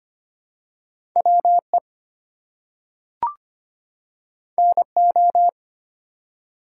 Morse code, heard as a keyed tone, about 25 wpm, 700 Hz. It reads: WE NO
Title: Two Letter Words - 25wpm